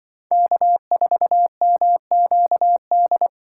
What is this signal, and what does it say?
Morse code, heard as a keyed tone, about 24 wpm, 700 Hz. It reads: K4MQD
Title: Speed Racing - USA 1x3 Call Signs 20wpm